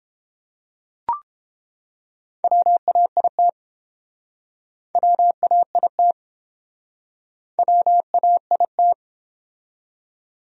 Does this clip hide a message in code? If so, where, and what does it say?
Yes: Morse code from 2.4 to 8.9 s, heard as a keyed tone, about 32 wpm, 700 Hz. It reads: WAIT WAIT WAIT